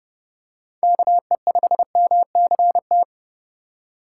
Morse code, heard as a keyed tone, about 30 wpm, 700 Hz. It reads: KE5MCT
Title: USA Call Signs - 6 Characters - 4x Character Spacing - 30wpm